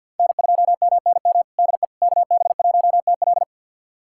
Morse code, heard as a keyed tone, about 52 wpm, 700 Hz. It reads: N2MNM DE KB1TL
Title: USA Callsign DE Callsign - 5 Characters - 50wpm